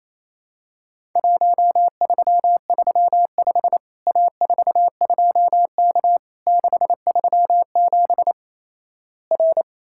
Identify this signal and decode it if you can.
Morse code, heard as a keyed tone, about 28 wpm, 700 Hz. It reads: 1335 A42K 637 R
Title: Worked All Europe DX Contest - QTC Exchange - 28wpm